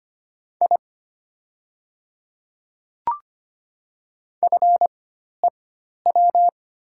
Morse code, heard as a keyed tone, about 25 wpm, spaced as for 12 wpm, 700 Hz. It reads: I FEW
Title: 4x Character Spacing - Top 300 Words - 25wpm